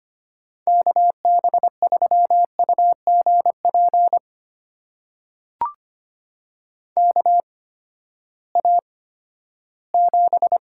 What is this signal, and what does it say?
Morse code, heard as a keyed tone, about 25 wpm, spaced as for 14 wpm, 700 Hz. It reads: KB3UGP KA7
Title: USA Call Signs - 8x Character Spacing - 25wpm